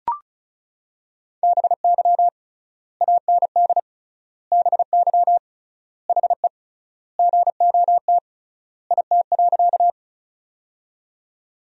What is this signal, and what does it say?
Morse code, heard as a keyed tone, about 36 wpm, 700 Hz. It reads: BY AND BY HE GOT IT.
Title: 3x Word-Spacing - Sentences from Top 300 Words - 35wpm